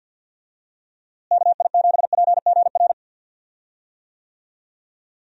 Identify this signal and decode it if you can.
Morse code, heard as a keyed tone, about 51 wpm, 700 Hz. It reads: KI7PGR
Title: USA Call Signs - 6 Characters - Repeated 3x - 50wpm